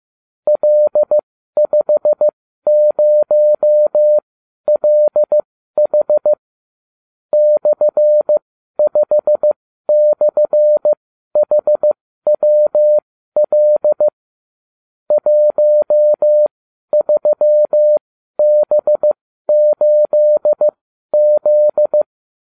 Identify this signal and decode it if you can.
Morse code, heard as a keyed tone, about 15 wpm, spaced as for 12 wpm, 600 Hz. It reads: L50LH /5/HWL 13B8Z